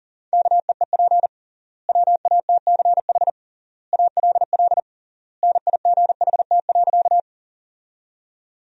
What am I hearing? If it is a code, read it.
Morse code, heard as a keyed tone, about 40 wpm, 700 Hz. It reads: KEEP WATCH ALL NIGHT.
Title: 3x Word-Spacing - Sentences from Top 300 Words - 40wpm